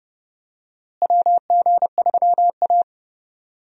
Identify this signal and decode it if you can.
Morse code, heard as a keyed tone, about 30 wpm, 700 Hz. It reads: WG3A
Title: USA Call Signs - 4 Characters - Repeated 3x - 30wpm